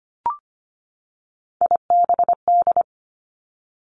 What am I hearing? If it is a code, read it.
Morse code, heard as a keyed tone, about 25 wpm, 700 Hz. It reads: IBD
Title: Three Challenging Letter-Number Combinations 25wpm